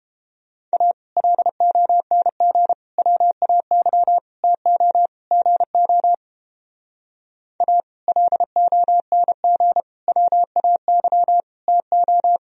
Morse code, heard as a keyed tone, about 29 wpm, 700 Hz. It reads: A LONG WAY TO GO A LONG WAY TO